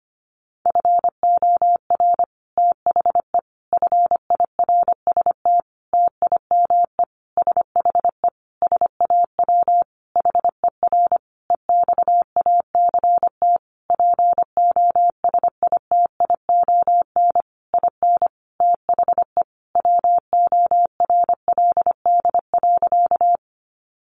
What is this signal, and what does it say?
Morse code, heard as a keyed tone, about 25 wpm, 700 Hz. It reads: FOR THE FIRST TIME SHE SAW HER EXACT POSITION IN THE WORLD.
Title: Sentences from Top 1300 Words 25wpm